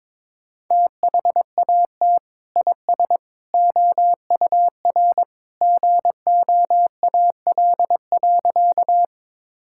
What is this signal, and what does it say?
Morse code, heard as a keyed tone, about 22 wpm, 700 Hz. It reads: THAT IS OUR GOAL.
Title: Sentences from Top 1000 Words - Review Part 5 - 22wpm